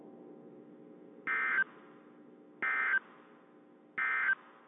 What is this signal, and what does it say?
Emergency Alert System (SAME) tones, heard as harsh buzzing bursts. It is heard from 1.2 to 4.3 s.
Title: WZ2523 - NOAA Weather Radio